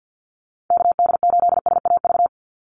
Morse code, heard as a keyed tone, about 50 wpm, 700 Hz. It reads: KD8SAV
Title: Call Signs - 50wpm